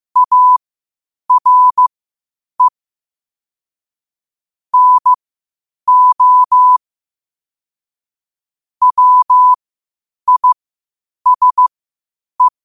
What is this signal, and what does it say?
Morse code, heard as a keyed tone, about 15 wpm, spaced as for 8 wpm, 1000 Hz. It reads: ARE NO WISE
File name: qotdfiles / 8wpm